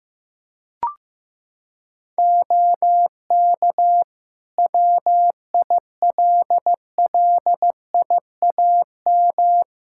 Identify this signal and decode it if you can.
Morse code, heard as a keyed tone, about 15 wpm, 700 Hz. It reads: OK WILLIAM
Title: Practice QSO - Exchange 2 - Info - 15wpm